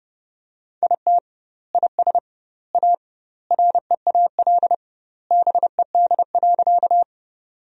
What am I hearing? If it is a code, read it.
Morse code, heard as a keyed tone, about 32 wpm, 700 Hz. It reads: IT IS A REAL BED.
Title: 2x Word-Spacing - Sentences from Top 500 Words - 30wpm